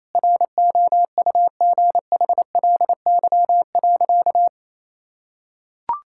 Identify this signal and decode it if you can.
Morse code, heard as a keyed tone, about 28 wpm, 700 Hz. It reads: ROUGHLY.
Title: Sentences from Top 1600 Words 28wpm